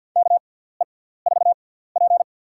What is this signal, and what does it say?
Morse code, heard as a keyed tone, about 52 wpm, spaced as for 17 wpm, 700 Hz. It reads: KE4P
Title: USA Call Signs - 4 Characters - 6x Character Spacing - 50wpm